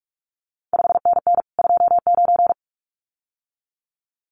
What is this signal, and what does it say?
Morse code, heard as a keyed tone, about 46 wpm, 700 Hz. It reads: 5NN 29
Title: CQ WW DX Contest Exchange Encore - 45wpm